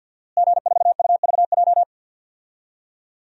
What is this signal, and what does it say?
Morse code, heard as a keyed tone, about 50 wpm, 700 Hz. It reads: G4UVJ
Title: DXCC-100 Call Signs - 5 Characters - Repeated 3x - 50wpm